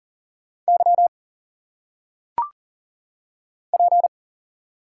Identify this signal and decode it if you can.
Morse code, heard as a keyed tone, about 40 wpm, 700 Hz. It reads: Y P
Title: Single Letters - Phonetic Pronunciation 40wpm